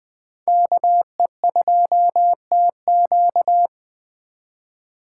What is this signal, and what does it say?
Morse code, heard as a keyed tone, about 20 wpm, 700 Hz. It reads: KE2TQ
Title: Speed Racing - USA Call Signs 17wpm